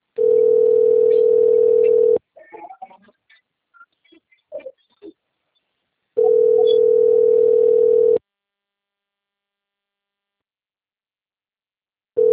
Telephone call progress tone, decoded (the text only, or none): ringback tone